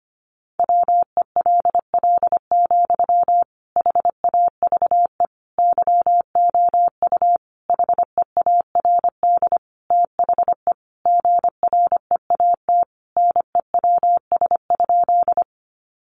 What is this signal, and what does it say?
Morse code, heard as a keyed tone, about 25 wpm, 700 Hz. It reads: WELL, HAVE YOU HEARD THE GREAT NEWS?
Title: Sentences from Top 800 Words 25wpm